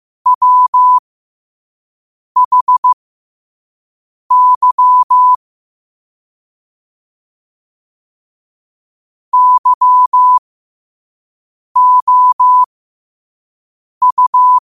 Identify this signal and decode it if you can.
Morse code, heard as a keyed tone, about 15 wpm, spaced as for 5 wpm, 1000 Hz. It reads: WHY YOU
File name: qotdfiles / 5wpm